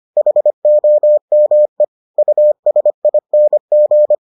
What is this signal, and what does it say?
Morse code, heard as a keyed tone, about 25 wpm, 600 Hz. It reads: HOME USING